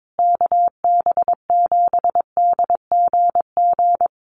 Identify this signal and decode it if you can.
Morse code, heard as a keyed tone, about 22 wpm, 700 Hz. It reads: KB7DGG
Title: USA Call Signs Repeated 2x 22wpm